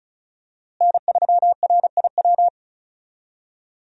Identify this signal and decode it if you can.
Morse code, heard as a keyed tone, about 34 wpm, 700 Hz. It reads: N3RIW
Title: USA Call Signs - 5 Characters - Repeated 3x - 35wpm